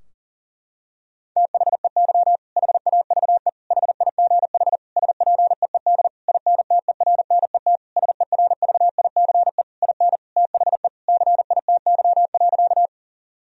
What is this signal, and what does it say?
Morse code, heard as a keyed tone, about 40 wpm, 700 Hz. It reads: THEY HAVE HIGH SPEED INTERNET SERVICE IN THE CITY.
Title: Sentences from Top 1000 Words - Review Part 4 - 40wpm